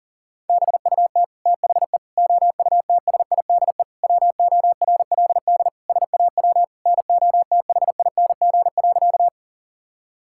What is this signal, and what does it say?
Morse code, heard as a keyed tone, about 40 wpm, 700 Hz. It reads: BUT THE OUTSIDE WORLD SAW NOTHING.
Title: Sentences from Top 1000 Words - Review Part 6 - 40wpm